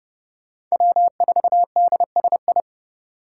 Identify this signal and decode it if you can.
Morse code, heard as a keyed tone, about 36 wpm, 700 Hz. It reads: W4DSI
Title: Speed Racing - USA Call Signs 30wpm